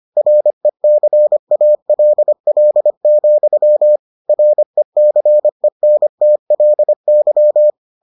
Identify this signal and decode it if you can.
Morse code, heard as a keyed tone, about 25 wpm, 600 Hz. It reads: RECALL, RECENTLY